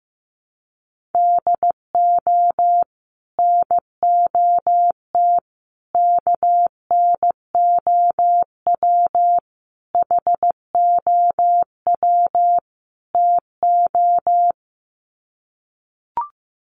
Morse code, heard as a keyed tone, about 15 wpm, 700 Hz. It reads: DO NOT KNOW HOW TO
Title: Sets of 5 Words 15wpm